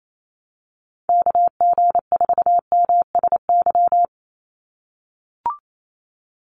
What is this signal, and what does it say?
Morse code, heard as a keyed tone, about 28 wpm, 700 Hz. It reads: KG4MSY
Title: USA Call Signs 28wpm